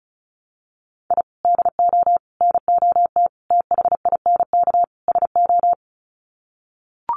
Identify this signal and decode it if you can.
Morse code, heard as a keyed tone, about 34 wpm, 700 Hz. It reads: I DO NOT THINK SO
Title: Sets of 5 Words 35wpm